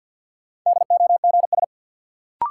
Morse code, heard as a keyed tone, about 46 wpm, 700 Hz. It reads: DOGS
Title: Sentences from Top 1600 Words - New Word Drill Alt - 50wpm